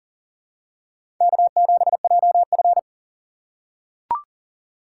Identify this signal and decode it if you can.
Morse code, heard as a keyed tone, about 40 wpm, 700 Hz. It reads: K7JF
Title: USA Call Signs - 4 Characters - 3x Character Spacing - 40wpm